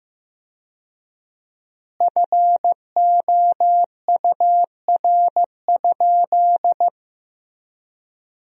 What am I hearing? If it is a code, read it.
Morse code, heard as a keyed tone, about 15 wpm, 700 Hz. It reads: FOUR?
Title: 4x Word-Spacing - Sentences from Top 300 Words - 15wpm